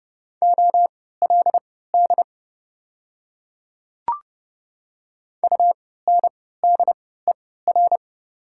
Morse code, heard as a keyed tone, about 31 wpm, spaced as for 17 wpm, 690 Hz. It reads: OLD UNDER
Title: 3x Character Spacing - Top 300 Words - 30wpm